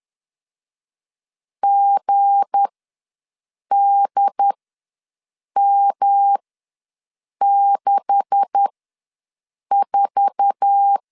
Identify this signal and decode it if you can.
Morse code, heard as a keyed tone, about 10 wpm, spaced as for 6 wpm, 800 Hz. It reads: GDM64
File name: LEZIONI / 9 286 ws 10wpm 4032